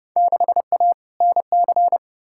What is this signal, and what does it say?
Morse code, heard as a keyed tone, about 30 wpm, 700 Hz. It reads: 6A NC
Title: Field Day Exchanges Encore - 30wpm